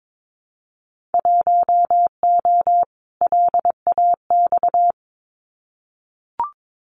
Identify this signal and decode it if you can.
Morse code, heard as a keyed tone, about 22 wpm, 700 Hz. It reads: 1O LAX
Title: Winter Field Day Exchanges 22wpm